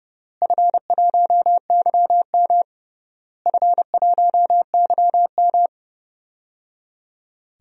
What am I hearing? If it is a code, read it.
Morse code, heard as a keyed tone, about 30 wpm, 700 Hz. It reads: F1YM F1YM
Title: DXCC-100 Call Signs - 4 Characters - Repeated 3x - 30wpm